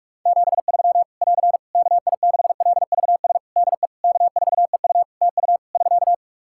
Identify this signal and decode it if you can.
Morse code, heard as a keyed tone, about 46 wpm, 700 Hz. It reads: 73 <AR> KI6LVS DE K4EU TU <SK>
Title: K1USN SST QSO Practice 45wpm